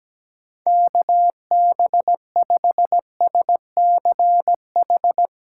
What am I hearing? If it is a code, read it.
Morse code, heard as a keyed tone, about 17 wpm, 700 Hz. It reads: KB5SCH
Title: USA Callsign DE Callsign - 6 Characters - 17wpm